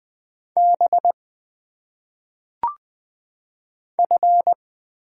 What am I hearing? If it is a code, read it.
Morse code, heard as a keyed tone, about 20 wpm, 700 Hz. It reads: B F
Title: Single Letters - Phonetic Pronunciation 20wpm